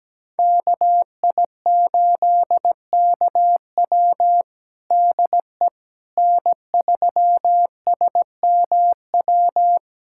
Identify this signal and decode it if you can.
Morse code, heard as a keyed tone, about 17 wpm, 700 Hz. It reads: KI8KW DE N3SMW
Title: USA Callsign DE Callsign - 5 Characters - 17wpm